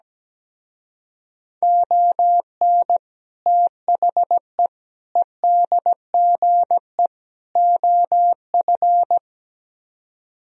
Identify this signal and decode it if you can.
Morse code, heard as a keyed tone, about 18 wpm, 700 Hz. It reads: ON THE EDGE OF
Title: Speed-Racing - Sets of 4 Words 17wpm